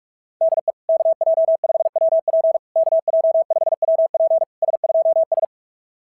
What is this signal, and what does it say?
Morse code, heard as a keyed tone, about 44 wpm, 660 Hz. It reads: DE KJ5WP KJ5WP S2S
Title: SOTA QSO Practice 45wpm